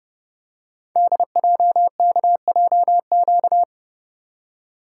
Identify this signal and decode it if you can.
Morse code, heard as a keyed tone, about 29 wpm, 700 Hz. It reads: DJKJQ
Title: Code Groups - 5 Letters - 30wpm